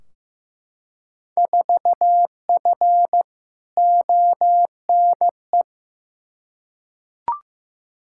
Morse code, heard as a keyed tone, about 16 wpm, 700 Hz. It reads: 4F ONE
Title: Field Day Exchanges Encore - 15wpm